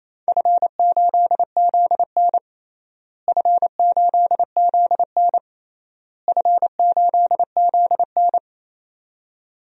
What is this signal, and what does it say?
Morse code, heard as a keyed tone, about 28 wpm, 700 Hz. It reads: F8ZN F8ZN F8ZN
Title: DXCC-100 Call Signs - 4 Characters - Repeated 3x - 28wpm